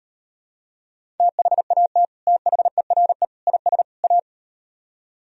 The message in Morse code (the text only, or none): THAT THERE IS A